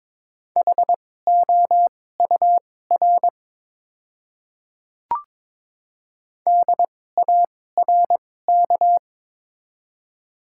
Morse code, heard as a keyed tone, about 23 wpm, spaced as for 16 wpm, 700 Hz. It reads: HOUR DARK
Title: 2x Character Spacing - Top 500 Words - 22wpm